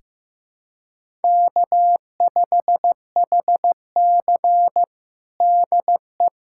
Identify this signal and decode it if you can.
Morse code, heard as a keyed tone, about 15 wpm, 700 Hz. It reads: K5HC DE
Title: USA Callsign DE Callsign - 4 Characters - 15wpm